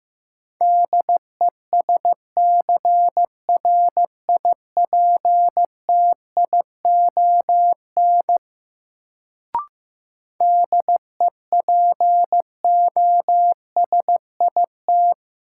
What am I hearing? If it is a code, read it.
Morse code, heard as a keyed tone, about 15 wpm, 700 Hz. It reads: DESCRIPTION DEPOSIT
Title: Sentences from Top 1300 Words - New Word Drill - 15wpm